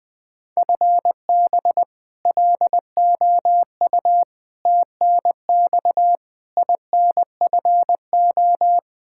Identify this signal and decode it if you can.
Morse code, heard as a keyed tone, about 20 wpm, 700 Hz. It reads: FB LOU TNX INFO
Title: K1USN SST QSO Practice 20wpm